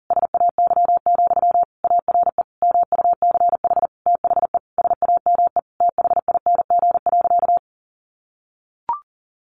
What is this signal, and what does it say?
Morse code, heard as a keyed tone, about 40 wpm, 700 Hz. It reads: SAY, ARE MUCH THE SAME THING.
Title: Sentences from Top 1100 Words 40wpm